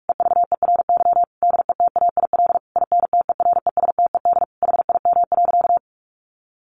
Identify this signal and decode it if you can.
Morse code, heard as a keyed tone, about 44 wpm, 700 Hz. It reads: EVERY DETAIL INTERESTED HIM.
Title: Sentences from Top 1500 Words 45wpm